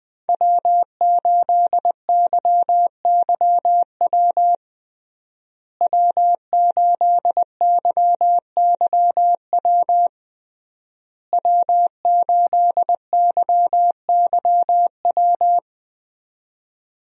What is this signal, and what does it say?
Morse code, heard as a keyed tone, about 20 wpm, 700 Hz. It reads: W8YYW W8YYW W8YYW
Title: USA Call Signs - 5 Characters - Repeated 3x - 20wpm